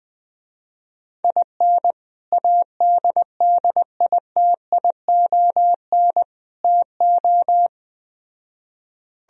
Morse code, heard as a keyed tone, about 20 wpm, 700 Hz. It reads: IN ADDITION TO
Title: Speed-Racing - Sets of 3 Words 17wpm